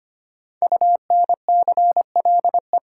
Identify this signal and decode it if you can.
Morse code, heard as a keyed tone, about 25 wpm, 700 Hz. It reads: UNCLE
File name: 5 Letter Words - First 3 Letters Given - 25wpm